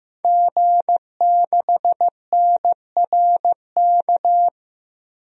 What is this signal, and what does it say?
Morse code, heard as a keyed tone, about 16 wpm, 700 Hz. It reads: G6NRK
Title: DXCC-100 Call Signs - 5 Characters - 15wpm